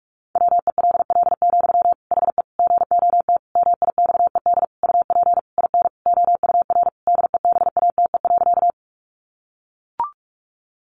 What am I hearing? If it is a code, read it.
Morse code, heard as a keyed tone, about 44 wpm, 700 Hz. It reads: WELL, HE GOT MIXED UP IN OUR DEBATE.
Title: Sentences from Top 1500 Words 45wpm